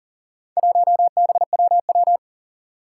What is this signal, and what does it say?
Morse code, heard as a keyed tone, about 42 wpm, 700 Hz. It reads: 1BWW
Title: Four Challenging Letter-Number Combinations 40wpm